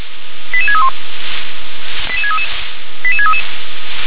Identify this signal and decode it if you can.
Five-tone selective call (ZVEI), heard as quick runs of five tones, tones of about 70 ms. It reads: 88631 88633 88633
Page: /357/SDR/ZVEI/